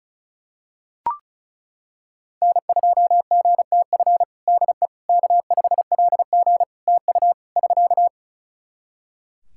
Morse code, heard as a keyed tone, about 36 wpm, 700 Hz. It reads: N2GTF DE K5LG TU <SK>
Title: Practice QSO - Exchange 4 - Ending - 35wpm